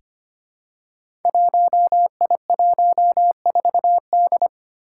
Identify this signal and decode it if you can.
Morse code, heard as a keyed tone, about 25 wpm, 700 Hz. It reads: 1I14D
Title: Five Challenging Letter-Number Combinations 25wpm